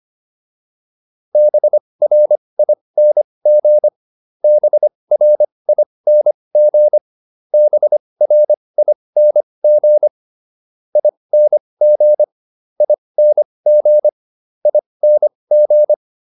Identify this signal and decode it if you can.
Morse code, heard as a keyed tone, about 25 wpm, spaced as for 20 wpm, 600 Hz. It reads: BRING BRING BRING ING ING ING